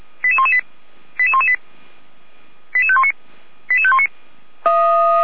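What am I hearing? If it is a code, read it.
Five-tone selective call (ZVEI), heard as quick runs of five tones, tones of about 80 ms. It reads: 88118 88118 88519 88519
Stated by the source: ZVEI five-tone call received on an SDR, from the feuerwehr-orlamuende.org fire brigade audio archive.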